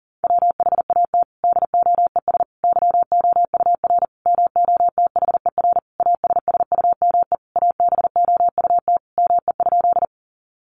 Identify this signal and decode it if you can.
Morse code, heard as a keyed tone, about 40 wpm, 700 Hz. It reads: WHAT DOES YOUR MOTHER ASSUME ABOUT ME?
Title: Sentences from Top 1500 Words 40wpm